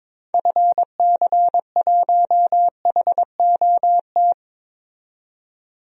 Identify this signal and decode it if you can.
Morse code, heard as a keyed tone, about 22 wpm, 700 Hz. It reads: FC1HOT
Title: DXCC-100 Call Signs - 6 Characters - 22wpm